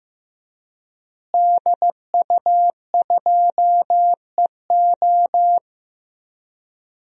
Morse code, heard as a keyed tone, about 15 wpm, 700 Hz. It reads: DU2EO
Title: DXCC-100 Call Signs - 5 Characters - Repeated 3x - 15wpm